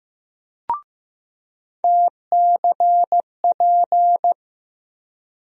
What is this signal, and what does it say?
Morse code, heard as a keyed tone, about 15 wpm, 700 Hz. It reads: TCP
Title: USA Call Sign Postfixes - 3 Letters 15wpm